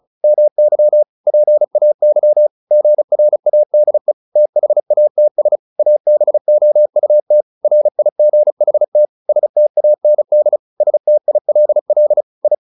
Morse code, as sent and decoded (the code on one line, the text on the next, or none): -- -.-- / .--. .- -.-- / --. .-. .- -.. / - .... .- - ... / .- -... --- ..- - / .-. .. --. .... - / ... - .- -. -.. / ... - .. .-.. .-.. / ..
MY PAY GRADE THATS ABOUT RIGHT STAND STILL I